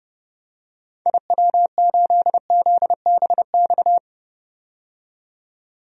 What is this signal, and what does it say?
Morse code, heard as a keyed tone, about 30 wpm, 700 Hz. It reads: IW8ZBX
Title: DXCC-100 Call Signs - 6 Characters - 30wpm